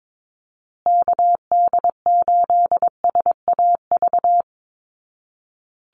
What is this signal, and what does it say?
Morse code, heard as a keyed tone, about 22 wpm, 700 Hz. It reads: KD8SAV
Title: Call Signs Repeated 2x - 22wpm